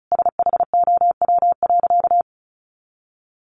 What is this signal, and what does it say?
Morse code, heard as a keyed tone, about 34 wpm, 700 Hz. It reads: SHOW.
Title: Sentences from Top 1400 Words 35wpm